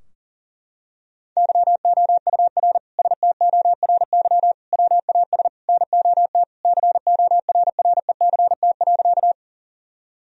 Morse code, heard as a keyed tone, about 40 wpm, 700 Hz. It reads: YOUR STORY WAS NOT CORRECT.
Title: Sentences from Top 1000 Words - Review Part 2 - 40wpm